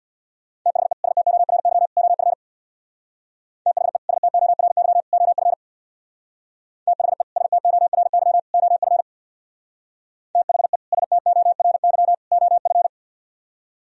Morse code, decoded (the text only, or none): THE STORY OF THE STORY OF THE STORY OF THE STORY OF